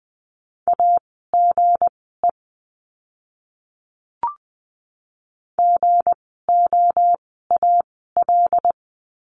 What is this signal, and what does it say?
Morse code, heard as a keyed tone, about 20 wpm, spaced as for 14 wpm, 700 Hz. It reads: AGE GOAL
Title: 160 Lesson 26f - Letter g - Words 20wpm